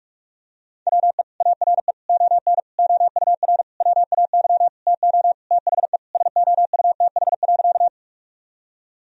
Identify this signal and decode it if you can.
Morse code, heard as a keyed tone, about 46 wpm, 700 Hz. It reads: WE ARE ON OUR WAY TO THE SOUTH.
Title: Sentences from Top 1000 Words - Review Part 2 - 45wpm